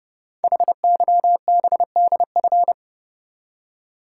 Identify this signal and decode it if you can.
Morse code, heard as a keyed tone, about 30 wpm, 700 Hz. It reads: HYBDF
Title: Code Groups - 5 Letter:Numbers - 30wpm